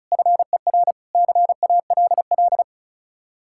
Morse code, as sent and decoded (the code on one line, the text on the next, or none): ..-. .-. / -.-. .- .-.. .-..
FER CALL